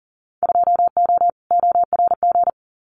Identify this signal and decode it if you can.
Morse code, heard as a keyed tone, about 38 wpm, 700 Hz. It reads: 2O ORG
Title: Winter Field Day Exchanges 40wpm